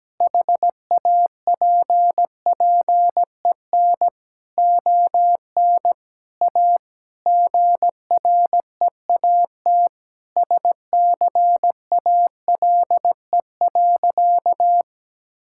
Morse code, heard as a keyed tone, about 17 wpm, 700 Hz. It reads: HAPPEN ON A GREAT SCALE.
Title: Sentences from Top 1000 Words - Review Part 5 - 17wpm